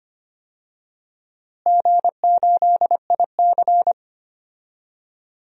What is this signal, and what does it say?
Morse code, heard as a keyed tone, about 25 wpm, 700 Hz. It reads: G8IC
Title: DXCC-100 Call Signs - 4 Characters - Repeated 2x - 25wpm